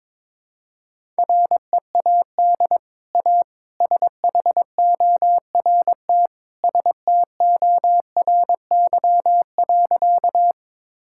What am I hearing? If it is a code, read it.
Morse code, heard as a keyed tone, about 22 wpm, 700 Hz. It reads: READ A SHORT STORY.
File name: practice / 22wpm / Easy Sentences with Hints - 22wpm